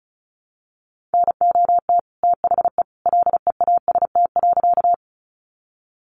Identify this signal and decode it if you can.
Morse code, heard as a keyed tone, about 34 wpm, 700 Hz. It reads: NOT THE LEAST.
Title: Sentences from Top 700 Words 35wpm